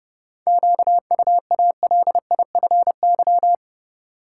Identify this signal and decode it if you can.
Morse code, heard as a keyed tone, about 32 wpm, 700 Hz. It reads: QUALIFY
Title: Seven Letter Words - 30wpm